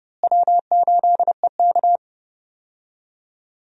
Morse code, heard as a keyed tone, about 31 wpm, 700 Hz. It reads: W8EK
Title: USA Call Signs - 4 Characters - Repeated 2x - 30wpm